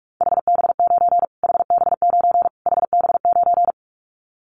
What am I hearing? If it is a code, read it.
Morse code, heard as a keyed tone, about 44 wpm, 700 Hz. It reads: HB9 HB9 HB9
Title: Callsign Prefixes for Top 25 DXCC - Repeated 3x - 45wpm